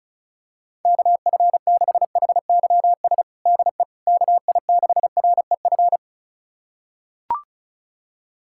Morse code, heard as a keyed tone, about 36 wpm, 690 Hz. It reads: KF6HYS DE KI6REF
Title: USA Callsign DE Callsign - 6 Characters - 35wpm